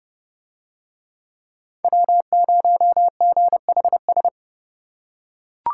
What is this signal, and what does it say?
Morse code, heard as a keyed tone, about 30 wpm, 700 Hz. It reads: W0GHS